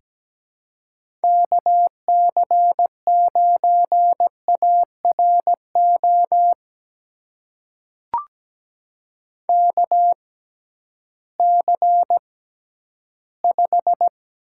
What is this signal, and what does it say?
Morse code, heard as a keyed tone, about 17 wpm, spaced as for 11 wpm, 700 Hz. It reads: KC9ARO KC5
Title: USA Call Signs - 6 Characters - 6x Character Spacing - 17wpm